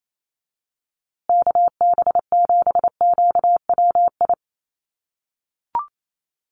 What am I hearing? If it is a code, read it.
Morse code, heard as a keyed tone, about 28 wpm, 690 Hz. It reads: KB7QWI
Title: USA Call Signs Repeated 3x 28wpm